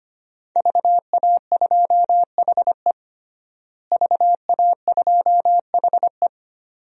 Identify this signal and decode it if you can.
Morse code, heard as a keyed tone, about 25 wpm, 700 Hz. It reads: VA2HE VA2HE
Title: DXCC-100 Call Signs - 5 Characters - Repeated 2x - 25wpm